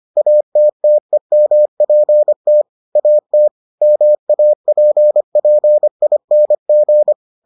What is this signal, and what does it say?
Morse code, heard as a keyed tone, about 25 wpm, 600 Hz. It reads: ATTEMPT AT MAPPING